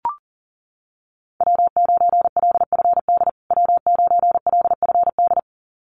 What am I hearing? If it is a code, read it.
Morse code, heard as a keyed tone, about 40 wpm, 700 Hz. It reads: W9LFD W9LFD
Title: USA Call Signs Repeated 2x 40wpm